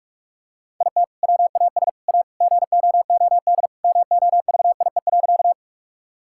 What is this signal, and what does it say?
Morse code, heard as a keyed tone, about 46 wpm, 700 Hz. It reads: IT WAS A GOOD MOVIE.
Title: Sentences from Top 1000 Words - Review Part 5 - 45wpm